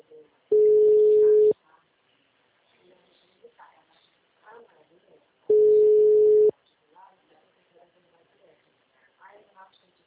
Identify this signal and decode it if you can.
Ringback tone (Europe, 425 Hz), with speech.